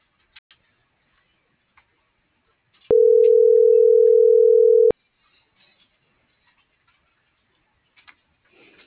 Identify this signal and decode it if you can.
Ringback tone (North America), 440 + 480 Hz.